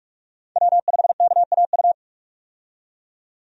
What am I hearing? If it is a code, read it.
Morse code, heard as a keyed tone, about 46 wpm, 700 Hz. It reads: W5KAU
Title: USA Call Signs - 5 Characters - 45wpm